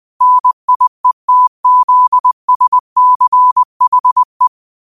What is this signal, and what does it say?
Morse code, heard as a keyed tone, about 20 wpm, 1000 Hz. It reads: NIETZSCHE